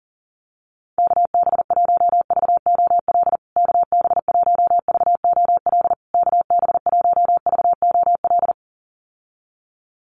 Morse code, heard as a keyed tone, about 37 wpm, 700 Hz. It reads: KB1VOL KB1VOL KB1VOL
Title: USA Call Signs Repeated 3x 40wpm